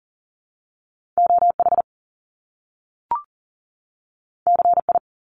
Morse code, heard as a keyed tone, about 42 wpm, 700 Hz. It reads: OH CI
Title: Top Two Letter Combinations in Words 40wpm